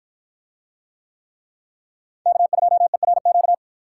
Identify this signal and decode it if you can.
Morse code, heard as a keyed tone, about 50 wpm, 700 Hz. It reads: K2ERQ